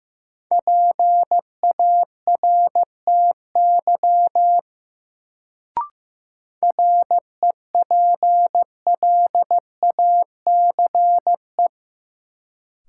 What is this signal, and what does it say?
Morse code, heard as a keyed tone, about 16 wpm, 700 Hz. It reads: PARTY REPLACE
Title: Sentences from Top 1400 Words - New Word Drill - 15wpm